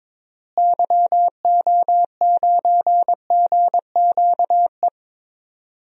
Morse code, heard as a keyed tone, about 21 wpm, 700 Hz. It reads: YO9GQE
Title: DXCC-100 Call Signs - 6 Characters - Repeated 2x - 22wpm